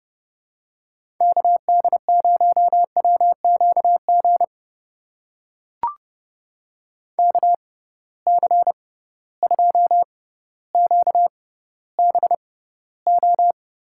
Morse code, heard as a keyed tone, about 32 wpm, spaced as for 16 wpm, 700 Hz. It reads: KD0WQG KC2QBO
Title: USA Call Signs - 6 Characters - 6x Character Spacing - 30wpm